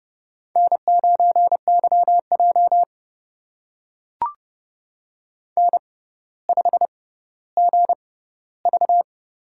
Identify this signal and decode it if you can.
Morse code, heard as a keyed tone, about 30 wpm, spaced as for 15 wpm, 700 Hz. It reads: N9YJ N5GV
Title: USA Call Signs - 4 Characters - 6x Character Spacing - 30wpm